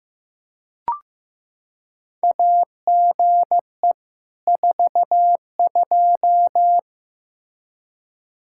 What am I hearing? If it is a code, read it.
Morse code, heard as a keyed tone, about 15 wpm, 700 Hz. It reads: AGE 42
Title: Practice QSO - Exchange 3 - More Info - 15wpm